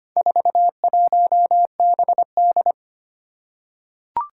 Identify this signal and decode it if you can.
Morse code, heard as a keyed tone, about 25 wpm, 700 Hz. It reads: 41BD
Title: Four Challenging Letter-Number Combinations 25wpm